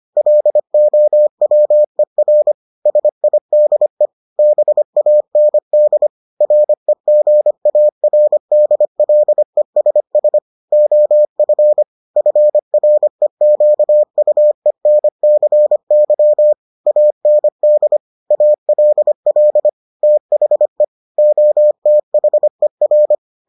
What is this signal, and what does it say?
Morse code, heard as a keyed tone, about 25 wpm, 600 Hz. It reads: LOWER SIDE BAND REGARDLESS OF FREQUENCY AND ALL THE OTHER